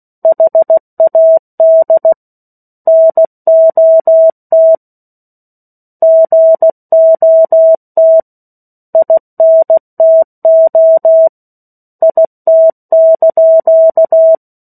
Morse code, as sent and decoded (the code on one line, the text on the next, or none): .... .- -.. / -. --- - / --. --- - / .. -. - --- / .. - -.--.-
HAD NOT GOT INTO IT)